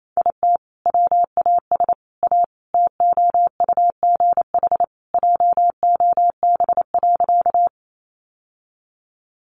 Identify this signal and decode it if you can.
Morse code, heard as a keyed tone, about 28 wpm, 700 Hz. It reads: IT WAS A TOUGH JOB.
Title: Sentences from Top 1200 Words 28wpm